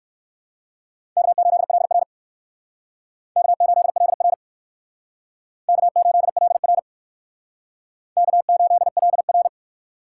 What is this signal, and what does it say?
Morse code, heard as a keyed tone, about 67 wpm, 700 Hz. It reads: K8LR K8LR K8LR K8LR